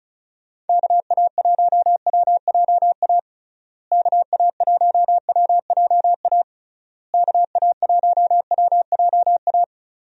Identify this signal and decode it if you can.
Morse code, heard as a keyed tone, about 34 wpm, 700 Hz. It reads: KA1WJA KA1WJA KA1WJA
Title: USA Call Signs - 6 Characters - Repeated 3x - 35wpm